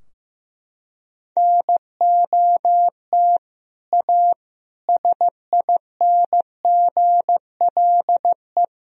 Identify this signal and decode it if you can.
Morse code, heard as a keyed tone, about 16 wpm, 700 Hz. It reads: NOT A SINGLE
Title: Sentences from Top 1000 Words - Review Part 4 - 15wpm